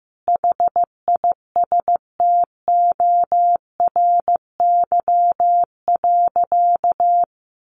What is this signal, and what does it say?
Morse code, heard as a keyed tone, about 15 wpm, 700 Hz. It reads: HISTORY.
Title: Sentences from Top 1300 Words 15wpm